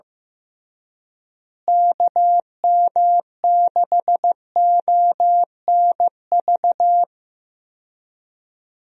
Morse code, heard as a keyed tone, about 15 wpm, 700 Hz. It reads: KM6ONV
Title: USA Call Signs - 6 Characters - Repeated 3x - 15wpm